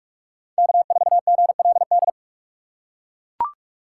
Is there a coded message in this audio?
Yes: Morse code from 0.6 to 2.1 s, heard as a keyed tone, about 41 wpm, 700 Hz. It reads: K4GLD